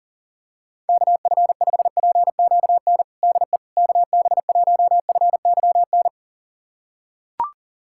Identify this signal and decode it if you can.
Morse code, heard as a keyed tone, about 40 wpm, 700 Hz. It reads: KF5PQN DE KB1FYN
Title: USA Callsign DE Callsign - 6 Characters - 40wpm